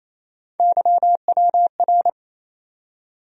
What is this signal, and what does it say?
Morse code, heard as a keyed tone, about 28 wpm, 700 Hz. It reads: YWR
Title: USA Call Sign Postfixes - 3 Letters 28wpm